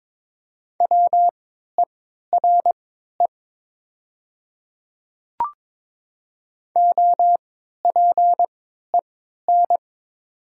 Morse code, heard as a keyed tone, about 21 wpm, spaced as for 12 wpm, 700 Hz. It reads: WERE OPEN
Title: 3x Character Spacing - Top 500 Words - 22wpm